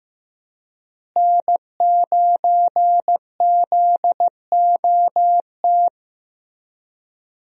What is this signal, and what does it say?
Morse code, heard as a keyed tone, about 15 wpm, 700 Hz. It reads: N9ZOT
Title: USA Call Signs - 5 Characters - Repeated 2x - 15wpm